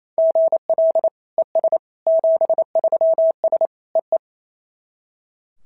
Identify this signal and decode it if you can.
Morse code, heard as a keyed tone, about 28 wpm, 650 Hz. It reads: GL ES 73S EE
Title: SOTA QSO Practice 28wpm